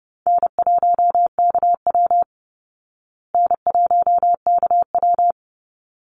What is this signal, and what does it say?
Morse code, heard as a keyed tone, about 30 wpm, 700 Hz. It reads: N1KW N1KW
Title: Easy Call Signs Repeated 2x - 30wpm